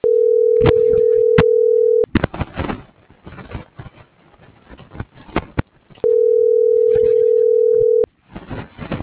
Ringback tone (North America), 440 + 480 Hz.